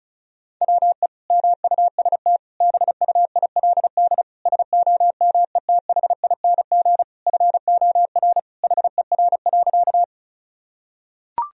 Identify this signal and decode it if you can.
Morse code, heard as a keyed tone, about 36 wpm, 700 Hz. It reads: WE MUST BUILD SOMETHING FOR HER.
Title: Sentences from Top 1000 Words - Review Part 1 - 35wpm